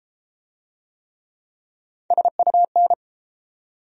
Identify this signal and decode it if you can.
Morse code, heard as a keyed tone, about 31 wpm, 700 Hz. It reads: SUN